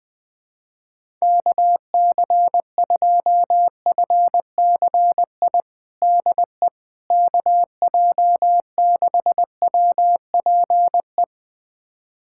Morse code, heard as a keyed tone, about 20 wpm, 700 Hz. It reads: KC2FCI DE KJ6WPE
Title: USA Callsign DE Callsign 20wpm